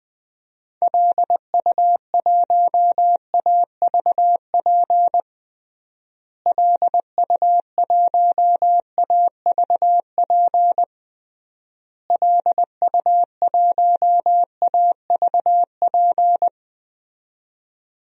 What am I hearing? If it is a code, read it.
Morse code, heard as a keyed tone, about 21 wpm, 700 Hz. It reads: LU1AVP LU1AVP LU1AVP
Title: DXCC-100 Call Signs - 6 Characters - Repeated 3x - 20wpm